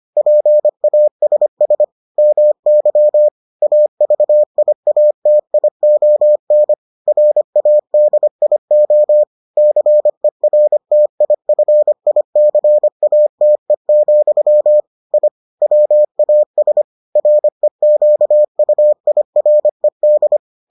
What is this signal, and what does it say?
Morse code, heard as a keyed tone, about 25 wpm, 600 Hz. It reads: PASS MY AVIATION RADIO CERTIFICATE, I WAS REQUIRED